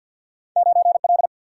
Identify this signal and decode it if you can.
Morse code, heard as a keyed tone, about 47 wpm, 700 Hz. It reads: 9 L
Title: Single-Letter Single-Number 50wpm